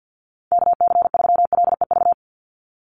Morse code, heard as a keyed tone, about 48 wpm, 700 Hz. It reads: KC3LEV
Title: USA Call Signs 50wpm